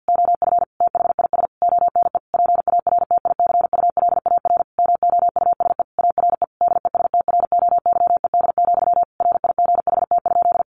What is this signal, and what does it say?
Morse code, heard as a keyed tone, about 50 wpm, 700 Hz. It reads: OF THIS ONE PARTICULAR MOUSE ARE DESTROYED, RIGHT?
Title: A Sound Of Thunder 50wpm